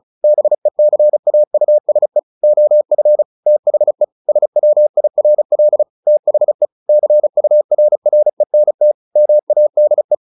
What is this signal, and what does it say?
Morse code, heard as a keyed tone, about 36 wpm, 600 Hz. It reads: BECAUSE OF THE SWIRL THE CURRENT MADE
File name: the old man and the sea / the old man and the sea 1 35wpm